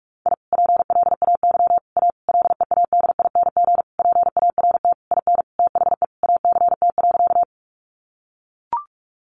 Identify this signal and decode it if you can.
Morse code, heard as a keyed tone, about 44 wpm, 700 Hz. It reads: I PLAY A LEADING PART IN THE ACT.
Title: Sentences from Top 1100 Words 45wpm